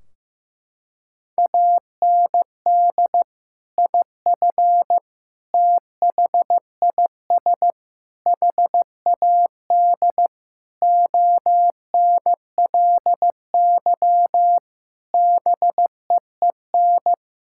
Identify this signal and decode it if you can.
Morse code, heard as a keyed tone, about 15 wpm, 700 Hz. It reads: AND IF THIS HAD ONLY BEEN
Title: Sentences from Top 1600 Words 15wpm